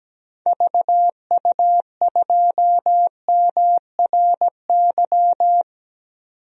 Morse code, heard as a keyed tone, about 17 wpm, 700 Hz. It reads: VU2MRY
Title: DXCC-100 Call Signs - 6 Characters - 17wpm